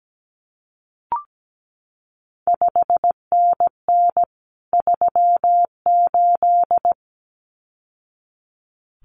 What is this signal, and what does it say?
Morse code, heard as a keyed tone, about 17 wpm, 700 Hz. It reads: 5NN 38
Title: CQ WW DX Contest Exchange Encore - 17wpm